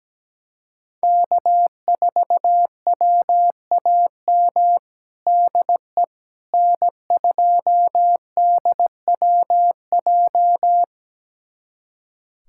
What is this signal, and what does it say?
Morse code, heard as a keyed tone, about 17 wpm, 700 Hz. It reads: K4WAM DE N2DWJ
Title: USA Callsign DE Callsign - 5 Characters - 17wpm